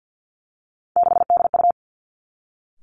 Morse code, heard as a keyed tone, about 50 wpm, 700 Hz. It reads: 6DU
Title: Three Challenging Letter-Number Combinations 50wpm